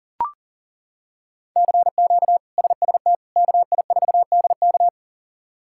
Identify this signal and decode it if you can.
Morse code, heard as a keyed tone, about 40 wpm, 700 Hz. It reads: CQ SST KI4DK
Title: K1USN SST QSO Practice - 40wpm